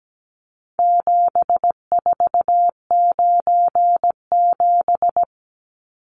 Morse code, heard as a keyed tone, about 17 wpm, 700 Hz. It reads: 7497
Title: Numbers - 4-Digit 17wpm